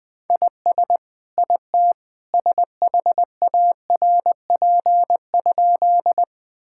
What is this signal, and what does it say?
Morse code, heard as a keyed tone, about 20 wpm, 700 Hz. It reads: IS IT SHARP?
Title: Sentences from Top 1600 Words 20wpm